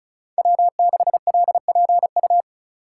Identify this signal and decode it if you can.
Morse code, heard as a keyed tone, about 36 wpm, 700 Hz. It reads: W6LPU
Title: USA Call Signs - 5 Characters - 2x Character Spacing - 35wpm